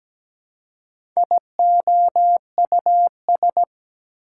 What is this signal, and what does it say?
Morse code, heard as a keyed tone, about 17 wpm, 700 Hz. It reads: IOUS
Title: Prefixes and Suffixes 17wpm